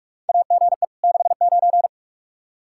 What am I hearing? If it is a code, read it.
Morse code, heard as a keyed tone, about 46 wpm, 700 Hz. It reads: AGE 69
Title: Practice QSO - Exchange 3 - More Info - 45wpm